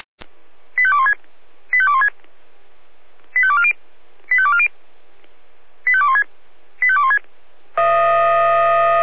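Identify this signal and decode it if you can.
Five-tone selective call (ZVEI), heard as quick runs of five tones, tones of about 70 ms. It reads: 85216 85216 85240 85240 85216 85216
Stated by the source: ZVEI five-tone selective call 85244 received on an SDR.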